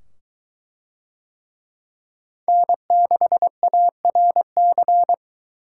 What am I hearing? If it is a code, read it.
Morse code, heard as a keyed tone, about 20 wpm, 700 Hz. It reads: N6ARC